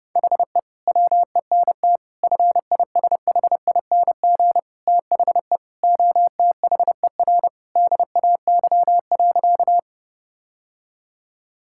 Morse code, heard as a keyed tone, about 30 wpm, 700 Hz. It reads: HE WENT FISHING THE OTHER DAY.
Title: Sentences from Top 1000 Words - Review Part 5 - 30wpm